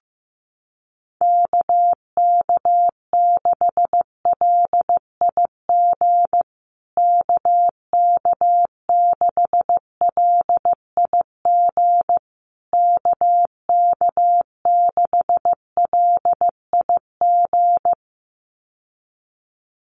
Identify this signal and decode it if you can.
Morse code, heard as a keyed tone, about 15 wpm, 700 Hz. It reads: KK6LIG KK6LIG KK6LIG
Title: USA Call Signs Repeated 3x 15wpm